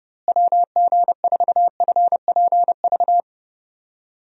Morse code, heard as a keyed tone, about 30 wpm, 700 Hz. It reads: WG4FPV
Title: USA Call Signs - 6 Characters - Repeated 2x - 30wpm